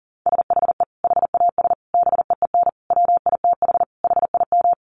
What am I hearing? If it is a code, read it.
Morse code, heard as a keyed tone, about 40 wpm, 700 Hz. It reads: SHE HAS BEEN WITH HIM
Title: Sentences from Top 200 Words 40wpm